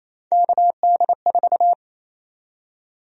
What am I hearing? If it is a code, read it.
Morse code, heard as a keyed tone, about 28 wpm, 700 Hz. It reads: KD4
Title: USA Call Sign Prefixes - Repeated 2x - 28wpm